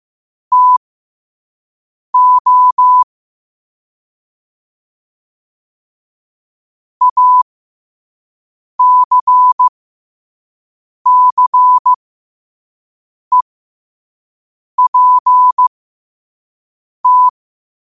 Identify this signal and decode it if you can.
Morse code, heard as a keyed tone, about 15 wpm, spaced as for 5 wpm, 1000 Hz. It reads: TO ACCEPT
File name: qotdfiles / 5wpm